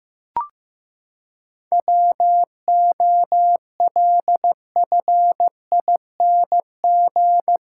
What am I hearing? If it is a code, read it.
Morse code, heard as a keyed tone, about 15 wpm, 700 Hz. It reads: WOLFING
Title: Pangram Challenge v2 - 15wpm